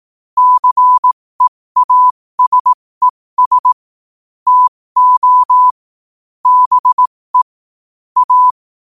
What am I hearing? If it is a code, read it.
Morse code, heard as a keyed tone, about 18 wpm, spaced as for 15 wpm, 1000 Hz. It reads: CEASES TO BE A
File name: qotdfiles / 15wpm